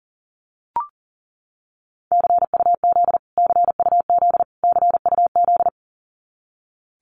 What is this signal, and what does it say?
Morse code, heard as a keyed tone, about 40 wpm, 700 Hz. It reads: CUZ CUZ CUZ
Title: Abbreviations Repeated 3x - 40wpm